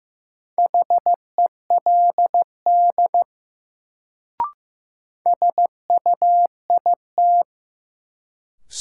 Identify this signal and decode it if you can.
Morse code, heard as a keyed tone, about 16 wpm, 700 Hz. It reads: HELD SUIT
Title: Sentences from Top 1600 Words - New Word Drill Alt - 15wpm